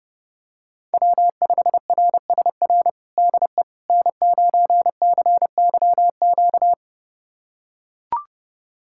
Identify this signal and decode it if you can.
Morse code, heard as a keyed tone, about 30 wpm, 700 Hz. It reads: W5RSR DE N9CYQ
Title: USA Callsign DE Callsign - 5 Characters - 30wpm